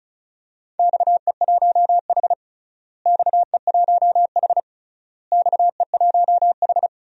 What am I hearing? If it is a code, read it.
Morse code, heard as a keyed tone, about 36 wpm, 690 Hz. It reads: XE1H XE1H XE1H
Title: DXCC-100 Call Signs - 4 Characters - Repeated 3x - 35wpm